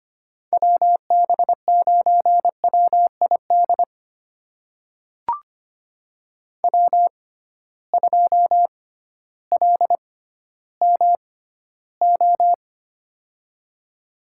Morse code, heard as a keyed tone, about 25 wpm, spaced as for 14 wpm, 700 Hz. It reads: WB9WID W2LMO
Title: USA Call Signs - 6x Character Spacing - 25wpm